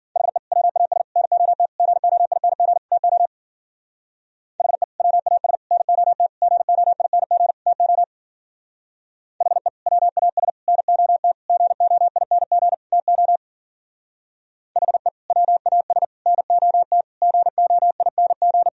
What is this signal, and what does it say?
Morse code, heard as a keyed tone, about 50 wpm, 700 Hz. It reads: HE WAS NOT GOING TO HE WAS NOT GOING TO HE WAS NOT GOING TO HE WAS NOT GOING
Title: Speed-Racing - Sets of 5 Words 40wpm